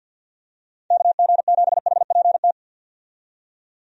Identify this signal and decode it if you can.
Morse code, heard as a keyed tone, about 53 wpm, 700 Hz. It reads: KG7HPT
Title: USA Call Signs - 6 Characters - 50wpm